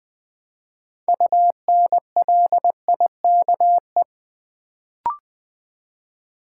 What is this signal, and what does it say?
Morse code, heard as a keyed tone, about 20 wpm, 690 Hz. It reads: UNLIKE
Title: Build A Word - 2 Letters At A Time - 6-Letter Words - 20wpm